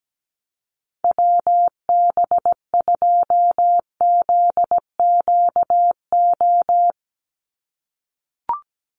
Morse code, heard as a keyed tone, about 17 wpm, 700 Hz. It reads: WB2ZQO
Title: USA Call Signs 17wpm